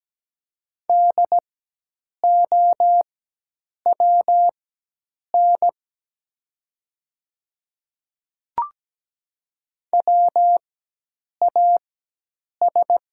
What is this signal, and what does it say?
Morse code, heard as a keyed tone, about 17 wpm, spaced as for 8 wpm, 700 Hz. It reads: DOWN WAS
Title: 4x Character Spacing - Top 500 Words - 17wpm